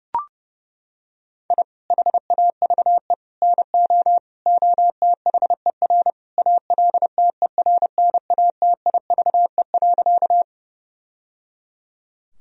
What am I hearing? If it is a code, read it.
Morse code, heard as a keyed tone, about 28 wpm, 700 Hz. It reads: I HAVE NO OTHER ALTERNATIVE.
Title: Sentences from Top 1000 Words - Review Part 4 - 30wpm